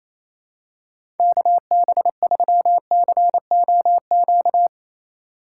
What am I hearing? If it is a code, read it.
Morse code, heard as a keyed tone, about 28 wpm, 700 Hz. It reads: KB3COQ
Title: USA Call Signs - 6 Characters - 3x Character Spacing - 28wpm